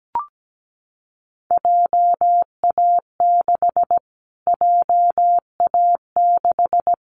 Morse code, heard as a keyed tone, about 17 wpm, 700 Hz. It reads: JA6 JA6
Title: Callsign Prefixes for Top 25 DXCC - Repeated 3x - 17wpm